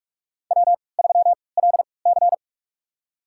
Morse code, heard as a keyed tone, about 46 wpm, spaced as for 26 wpm, 700 Hz. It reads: W3LC
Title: USA Call Signs - 4 Characters - 3x Character Spacing - 45wpm